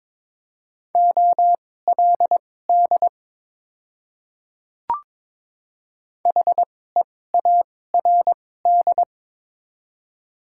Morse code, heard as a keyed tone, about 22 wpm, spaced as for 16 wpm, 700 Hz. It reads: OLD HEARD
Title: 2x Character Spacing - Top 500 Words - 22wpm